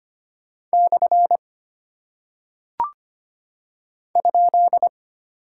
Morse code, heard as a keyed tone, about 25 wpm, 700 Hz. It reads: / ?
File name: Single Punctuation-Prosign - 25wpm